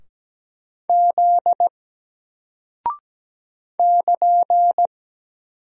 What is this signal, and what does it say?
Morse code, heard as a keyed tone, about 17 wpm, 700 Hz. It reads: Z <KN>
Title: Single Character (Letter:Number:Punctuation:Prosign) Alt 17wpm